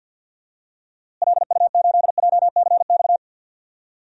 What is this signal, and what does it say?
Morse code, heard as a keyed tone, about 49 wpm, 700 Hz. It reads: LU8JCX
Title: DXCC-100 Call Signs - 6 Characters - Repeated 3x - 50wpm